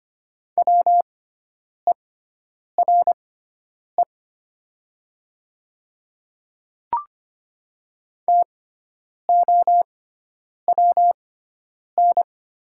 Morse code, heard as a keyed tone, about 25 wpm, spaced as for 9 wpm, 700 Hz. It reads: WERE TOWN
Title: 6x Character Spacing - Top 500 Words - 25wpm